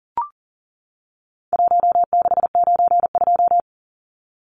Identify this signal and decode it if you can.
Morse code, heard as a keyed tone, about 40 wpm, 700 Hz. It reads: 1692
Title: Numbers - 4-Digit 40wpm